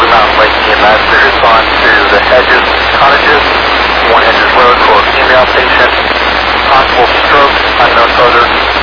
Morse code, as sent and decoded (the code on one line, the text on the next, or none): none